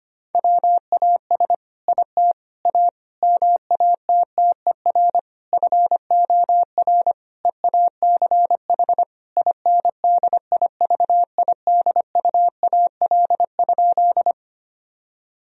Morse code, heard as a keyed tone, about 25 wpm, 700 Hz. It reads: WAS IT A MATTER FOR EACH INDIVIDUAL?
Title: Sentences from Top 1000 Words - Review Part 4 - 25wpm